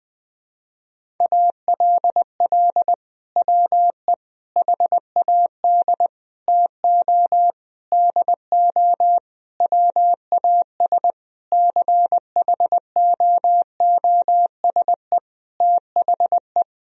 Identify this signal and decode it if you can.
Morse code, heard as a keyed tone, about 21 wpm, 700 Hz. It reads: ALL WE HAD TO DO WAS CHOOSE THE
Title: Sentences from Top 1000 Words - Review Part 4 - 20wpm